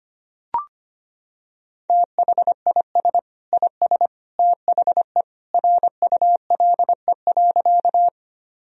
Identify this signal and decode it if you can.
Morse code, heard as a keyed tone, about 25 wpm, 700 Hz. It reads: THIS IS THE RULE.
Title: Sentences from Top 1000 Words - Review Part 2 - 25wpm